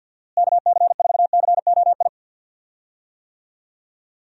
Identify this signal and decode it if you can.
Morse code, heard as a keyed tone, about 52 wpm, 700 Hz. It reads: KC4COI
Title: USA Call Signs - 6 Characters - Repeated 2x - 50wpm